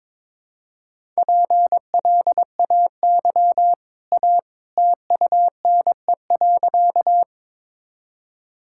Morse code, heard as a keyed tone, about 22 wpm, 700 Hz. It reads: PLAY A TUNE.
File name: practice / 22wpm / Easy Sentences with Hints - 22wpm